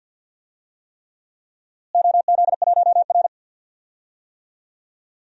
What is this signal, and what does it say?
Morse code, heard as a keyed tone, about 51 wpm, 700 Hz. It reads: OZ1R
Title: DXCC-100 Call Signs - 4 Characters - Repeated 3x - 50wpm